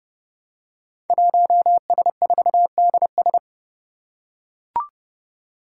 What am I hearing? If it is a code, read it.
Morse code, heard as a keyed tone, about 30 wpm, 700 Hz. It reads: 1S4DS
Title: Five Challenging Letter-Number Combinations 30wpm